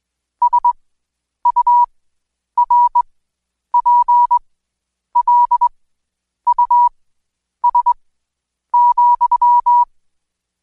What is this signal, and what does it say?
Morse code, heard as a keyed tone, about 21 wpm, spaced as for 9 wpm, 970 Hz. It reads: SURPLUS,
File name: Code Quick / AudioDownld2 / Seg115-Day20-DayDrill 20